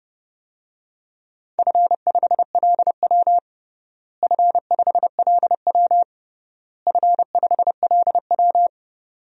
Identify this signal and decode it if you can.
Morse code, heard as a keyed tone, about 32 wpm, 700 Hz. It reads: F5LW F5LW F5LW
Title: DXCC-100 Call Signs - 4 Characters - Repeated 3x - 30wpm